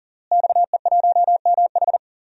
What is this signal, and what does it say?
Morse code, heard as a keyed tone, about 40 wpm, 700 Hz. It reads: XE1MH
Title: DXCC-100 Call Signs - 5 Characters - Repeated 3x - 40wpm